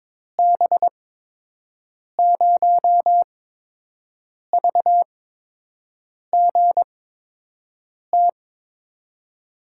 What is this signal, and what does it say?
Morse code, heard as a keyed tone, about 22 wpm, spaced as for 6 wpm, 700 Hz. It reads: B0VGT
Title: USA Call Signs - 6 Characters - 8x Character Spacing - 22wpm